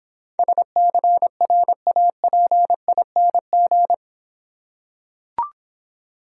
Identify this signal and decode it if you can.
Morse code, heard as a keyed tone, about 26 wpm, 700 Hz. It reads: SCRAPING